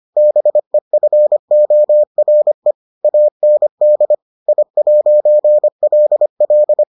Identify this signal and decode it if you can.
Morse code, heard as a keyed tone, about 25 wpm, 600 Hz. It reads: BEFORE AND I'LL